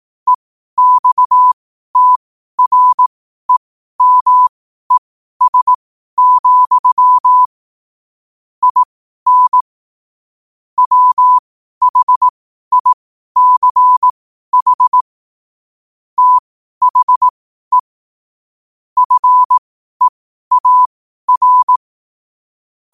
Morse code, heard as a keyed tone, about 18 wpm, spaced as for 12 wpm, 1000 Hz. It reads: EXTREMES, IN WHICH THE FEAR
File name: qotdfiles / 12wpm